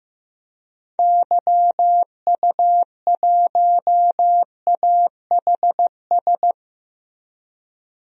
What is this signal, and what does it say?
Morse code, heard as a keyed tone, about 15 wpm, 700 Hz. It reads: YU1AHS
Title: DXCC-100 Call Signs - 6 Characters - Repeated 3x - 15wpm